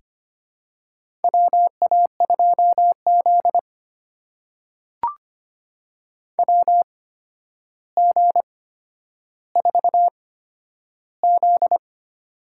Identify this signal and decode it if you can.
Morse code, heard as a keyed tone, about 25 wpm, spaced as for 11 wpm, 700 Hz. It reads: WA2Z WG4Z
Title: USA Call Signs - 4 Characters - 8x Character Spacing - 25wpm